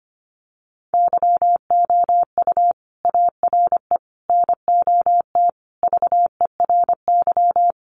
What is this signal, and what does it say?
Morse code, heard as a keyed tone, about 25 wpm, 700 Hz. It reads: YOU ARE NOT VERY
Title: Sentences from Top 1200 Words 25wpm